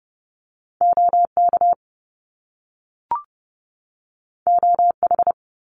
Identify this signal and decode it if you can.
Morse code, heard as a keyed tone, about 30 wpm, 700 Hz. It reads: OK OH
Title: Top Two Letter Combinations in Words 30wpm